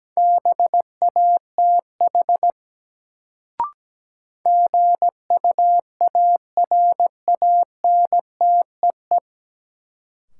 Morse code, heard as a keyed tone, about 17 wpm, 700 Hz. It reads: BATH GUARANTEE
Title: Sentences from Top 1600 Words - New Word Drill - 17wpm